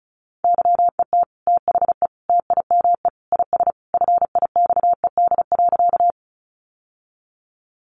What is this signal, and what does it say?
Morse code, heard as a keyed tone, about 34 wpm, 700 Hz. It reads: YET THE TIME IS FIXED.
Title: Sentences from Top 1100 Words 35wpm